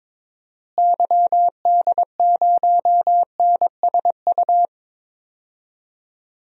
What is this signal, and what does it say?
Morse code, heard as a keyed tone, about 22 wpm, 700 Hz. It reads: YD0NSU
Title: DXCC-100 Call Signs - 6 Characters - Repeated 3x - 22wpm